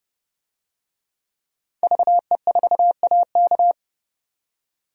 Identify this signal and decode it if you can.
Morse code, heard as a keyed tone, about 32 wpm, 700 Hz. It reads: VE4AK
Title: DXCC-100 Call Signs - 5 Characters - 30wpm